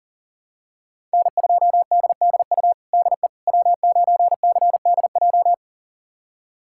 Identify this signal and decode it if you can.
Morse code, heard as a keyed tone, about 37 wpm, 700 Hz. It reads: N2DDU DE W9CDJ
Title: USA Callsign DE Callsign - 5 Characters - 40wpm